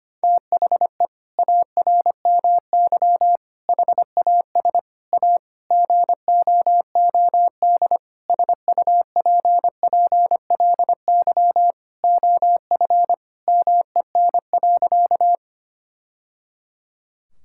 Morse code, heard as a keyed tone, about 25 wpm, 700 Hz. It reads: THE ARMY HAS A GOOD SUPPLY OF MEN.
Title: Sentences from Top 1000 Words - Review Part 5 - 25wpm